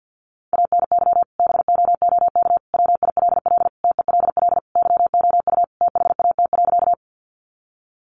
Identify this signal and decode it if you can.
Morse code, heard as a keyed tone, about 48 wpm, 700 Hz. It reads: ANY BOOK WILL TELL YOU THAT.
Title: Sentences from Top 300 Words 50wpm